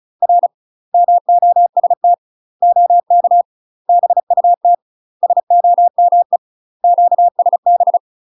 Morse code, heard as a keyed tone, about 36 wpm, 700 Hz. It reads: R MOST OK BUT SOME QSB
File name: UK MASTER QSOs TRAINING / 35WPM / UK MASTER QSOs 0007